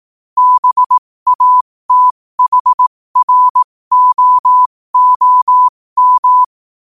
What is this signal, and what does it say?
Morse code, heard as a keyed tone, about 18 wpm, spaced as for 15 wpm, 1000 Hz. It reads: BATHROOM